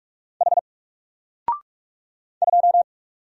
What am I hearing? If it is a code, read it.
Morse code, heard as a keyed tone, about 44 wpm, 700 Hz. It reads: H 2
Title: Single Character (Letter:Number:Punctuation:Prosign) Alt 45wpm